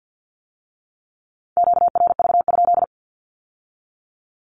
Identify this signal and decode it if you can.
Morse code, heard as a keyed tone, about 50 wpm, 700 Hz. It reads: QRV?
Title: Q-Codes with Spoken Meaning - 50wpm